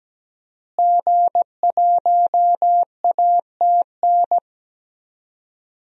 Morse code, heard as a keyed tone, about 17 wpm, 700 Hz. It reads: G1ATN
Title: DXCC-100 Call Signs - 5 Characters - Repeated 3x - 17wpm